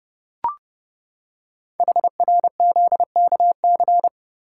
Morse code, heard as a keyed tone, about 29 wpm, 700 Hz. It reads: HRZKC
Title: Code Groups - 5 Letters - 30wpm